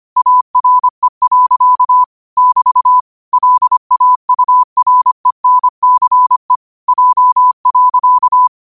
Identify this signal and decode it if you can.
Morse code, heard as a keyed tone, about 25 wpm, 1000 Hz. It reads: ARE. <BT> LAURENCE J.
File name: qotdfiles / 25wpm